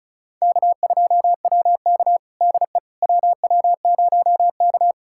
Morse code, heard as a keyed tone, about 36 wpm, 700 Hz. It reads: K2WK DE WW0K
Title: USA Callsign DE Callsign - 4 Characters - 35wpm